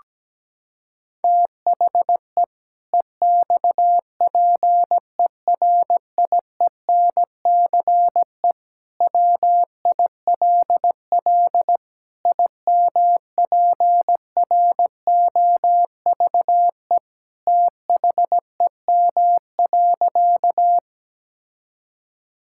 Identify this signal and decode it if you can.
Morse code, heard as a keyed tone, about 17 wpm, 700 Hz. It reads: THE EXPERIENCE WILL IMPROVE THEM.
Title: Sentences from Top 1000 Words - Review Part 5 - 17wpm